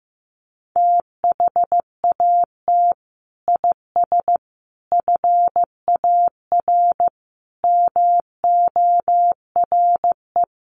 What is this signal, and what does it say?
Morse code, heard as a keyed tone, about 15 wpm, 700 Hz. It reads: THAT IS FAR MORE
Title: Sentences from Top 1200 Words 15wpm